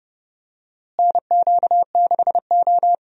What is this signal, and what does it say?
Morse code, heard as a keyed tone, about 30 wpm, 700 Hz. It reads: NQ6O
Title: USA Call Signs - 4 Characters - Repeated 3x - 30wpm